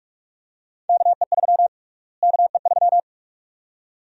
Morse code, heard as a keyed tone, about 46 wpm, 700 Hz. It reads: KE3 KE3
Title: USA Call Sign Prefixes - Repeated 2x - 45wpm